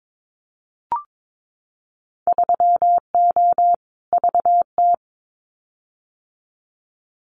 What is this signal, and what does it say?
Morse code, heard as a keyed tone, about 22 wpm, 700 Hz. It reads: 3O VT
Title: Winter Field Day Exchanges Encore 22wpm